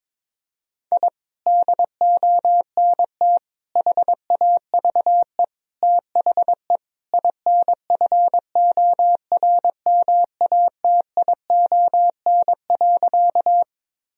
Morse code, heard as a keyed tone, about 22 wpm, 700 Hz. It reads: I DONT HAVE THE INFORMATION.
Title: Sentences from Top 1000 Words - Review Part 3 - 22wpm